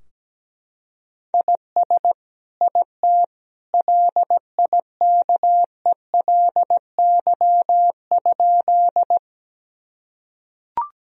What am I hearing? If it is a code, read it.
Morse code, heard as a keyed tone, about 17 wpm, 700 Hz. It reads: IS IT LIKELY?
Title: Sentences from Top 1000 Words - Review Part 4 - 17wpm